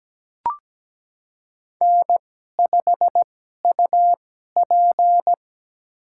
Morse code, heard as a keyed tone, about 17 wpm, spaced as for 12 wpm, 700 Hz. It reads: N5UP
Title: USA Call Signs - 4 Characters - 2x Character Spacing - 17wpm